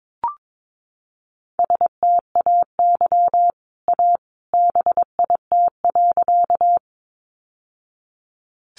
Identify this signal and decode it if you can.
Morse code, heard as a keyed tone, about 22 wpm, 700 Hz. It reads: STAY A BIT.
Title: Sentences from Top 900 Words 22wpm